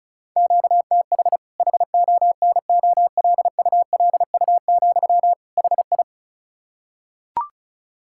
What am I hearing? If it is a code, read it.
Morse code, heard as a keyed tone, about 36 wpm, 700 Hz. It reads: QTH HONOLULU, HI
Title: K1USN SST QSO Practice 35wpm